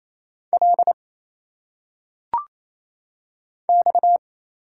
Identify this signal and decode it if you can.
Morse code, heard as a keyed tone, about 27 wpm, 700 Hz. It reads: L X
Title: Single Letters - Phonetic Pronunciation 28wpm